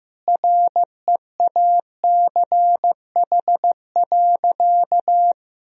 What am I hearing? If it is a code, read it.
Morse code, heard as a keyed tone, about 16 wpm, 700 Hz. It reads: REACH.
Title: Sentences from Top 1000 Words - Review Part 2 - 15wpm